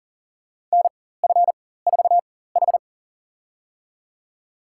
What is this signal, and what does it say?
Morse code, heard as a keyed tone, about 40 wpm, spaced as for 19 wpm, 700 Hz. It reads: NF4H
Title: USA Call Signs - 4 Characters - 4x Character Spacing - 40wpm